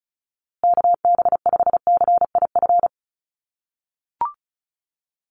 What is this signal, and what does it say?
Morse code, heard as a keyed tone, about 34 wpm, 700 Hz. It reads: KB5CIF
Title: USA Call Signs Repeated 3x 35wpm